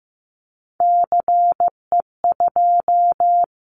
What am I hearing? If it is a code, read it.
Morse code, heard as a keyed tone, about 15 wpm, 700 Hz. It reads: CE2
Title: Callsign Prefixes for Top 100 DXCC - 15wpm